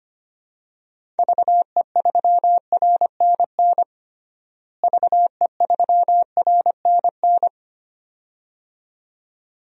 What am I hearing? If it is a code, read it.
Morse code, heard as a keyed tone, about 25 wpm, 700 Hz. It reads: VE3RNN VE3RNN
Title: DXCC-100 Call Signs - 6 Characters - Repeated 2x - 25wpm